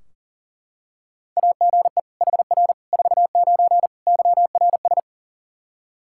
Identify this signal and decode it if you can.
Morse code, heard as a keyed tone, about 40 wpm, 700 Hz. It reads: AGE HR 49 YRS
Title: Practice QSO - Exchange 3 - More Info - 40wpm